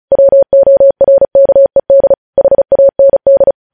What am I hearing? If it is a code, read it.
Morse code, heard as a keyed tone, about 36 wpm, 550 Hz. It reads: WORKED HAND
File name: podcast / News headlines-1x-35x35